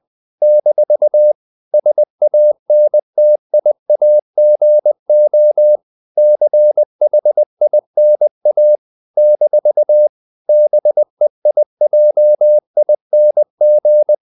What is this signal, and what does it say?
Morse code, heard as a keyed tone, about 20 wpm, 600 Hz. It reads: - SANTIAGO CHINA - BEIJING
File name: CW / Practice / NationalCapitals20